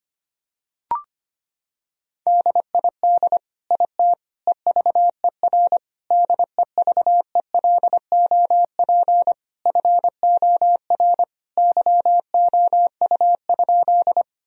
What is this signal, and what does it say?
Morse code, heard as a keyed tone, about 25 wpm, 700 Hz. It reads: DID IT EVER DEVELOP FOR YOU?
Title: Sentences from Top 1000 Words - Review Part 2 - 25wpm